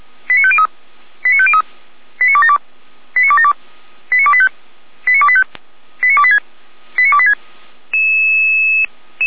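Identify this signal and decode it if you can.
Five-tone selective call (ZVEI), heard as quick runs of five tones, tones of about 70 ms. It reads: 89503 89503 89272 89272 89285 89285 89286 89286
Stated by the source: ZVEI five-tone selective call received by SDR, logged by Feuerwehr Orlamünde.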